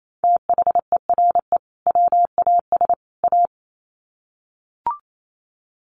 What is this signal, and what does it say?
Morse code, heard as a keyed tone, about 28 wpm, 700 Hz. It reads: THERE WAS A
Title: Sets of 3 Words 28wpm